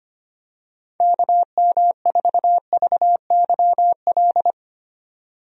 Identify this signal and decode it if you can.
Morse code, heard as a keyed tone, about 25 wpm, 700 Hz. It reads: KM4VYL
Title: USA Call Signs - 6 Characters - 25wpm